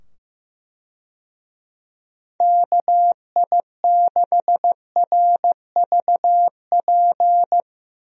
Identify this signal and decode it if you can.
Morse code, heard as a keyed tone, about 15 wpm, 700 Hz. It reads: KI6RVP
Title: USA Call Signs - 6 Characters - 15wpm